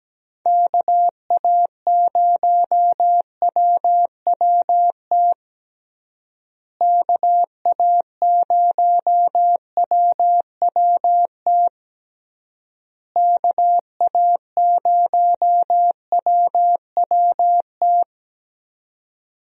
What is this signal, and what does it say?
Morse code, heard as a keyed tone, about 17 wpm, 700 Hz. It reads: KA0WWT KA0WWT KA0WWT
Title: USA Call Signs - 6 Characters - Repeated 3x - 17wpm